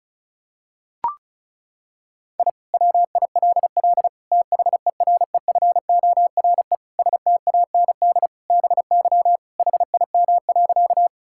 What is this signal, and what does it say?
Morse code, heard as a keyed tone, about 36 wpm, 700 Hz. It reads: I WILL THEREFORE STAND BY HIM.
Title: Sentences from Top 1000 Words - Review Part 5 - 35wpm